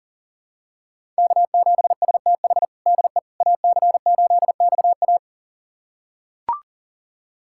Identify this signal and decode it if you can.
Morse code, heard as a keyed tone, about 40 wpm, 700 Hz. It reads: K7STH DE AC8XA
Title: USA Callsign DE Callsign - 5 Characters - 40wpm